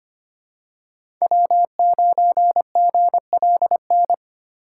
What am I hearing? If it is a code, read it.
Morse code, heard as a keyed tone, about 25 wpm, 700 Hz. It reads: W9GLN
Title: USA Call Signs - 5 Characters - 6x Character Spacing - 25wpm